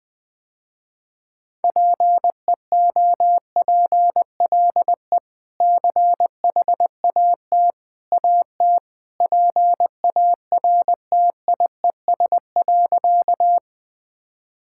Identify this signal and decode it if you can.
Morse code, heard as a keyed tone, about 20 wpm, 700 Hz. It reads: PEOPLE CHAT AT PARTIES.
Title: Easy Sentences with Hints - 20wpm